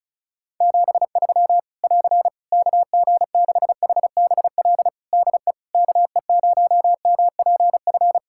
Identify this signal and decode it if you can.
Morse code, heard as a keyed tone, about 36 wpm, 700 Hz. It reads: 73 <AR> KG6HBL DE KE0MPF
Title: Practice QSO - Exchange 4 - Ending - 35wpm